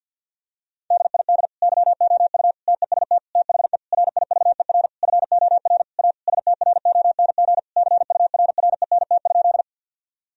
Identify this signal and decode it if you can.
Morse code, heard as a keyed tone, about 50 wpm, 700 Hz. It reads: DID YOU TEST THE RIVER FOR A STRONG CURRENT?
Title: Sentences from Top 1000 Words - Review Part 4 - 50wpm